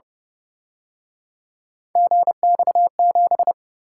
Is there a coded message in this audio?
Yes: Morse code from 2.0 to 3.5 s, heard as a keyed tone, about 31 wpm, 700 Hz. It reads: GX7